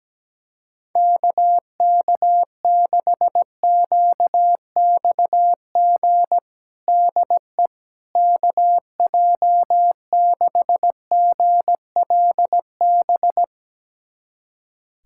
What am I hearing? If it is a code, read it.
Morse code, heard as a keyed tone, about 17 wpm, 700 Hz. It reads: KK6QXG DE KJ6GLB
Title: USA Callsign DE Callsign - 6 Characters - 17wpm